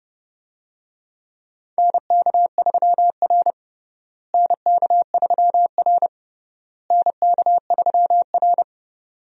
Morse code, heard as a keyed tone, about 30 wpm, 700 Hz. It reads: NK3R NK3R NK3R
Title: USA Call Signs - 4 Characters - Repeated 3x - 30wpm